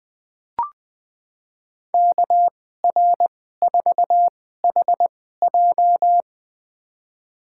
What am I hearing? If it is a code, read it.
Morse code, heard as a keyed tone, about 20 wpm, spaced as for 14 wpm, 700 Hz. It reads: KR4HJ
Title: USA Call Signs - 5 Characters - 2x Character Spacing - 20wpm